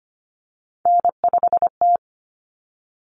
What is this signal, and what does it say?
Morse code, heard as a keyed tone, about 25 wpm, 700 Hz. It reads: N5T
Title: Common Cut Numbers - 3-Digit 25wpm